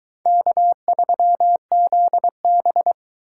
Morse code, heard as a keyed tone, about 22 wpm, 700 Hz. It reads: K3ZB